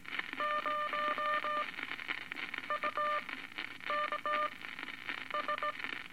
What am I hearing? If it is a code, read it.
Morse code, heard as a keyed tone, about 18 wpm, spaced as for 9 wpm, 1200 Hz, with radio noise. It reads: 0 UK S
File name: Eight Words Per Minute